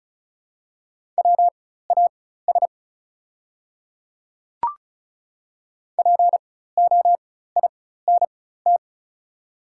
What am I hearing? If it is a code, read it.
Morse code, heard as a keyed tone, about 34 wpm, spaced as for 16 wpm, 700 Hz. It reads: WAS POINT
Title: 4x Character Spacing - Top 200 Words - 35wpm